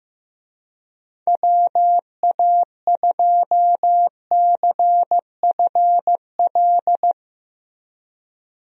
Morse code, heard as a keyed tone, about 16 wpm, 700 Hz. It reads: WA2CFL
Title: USA Call Signs - 6 Characters - Repeated 3x - 15wpm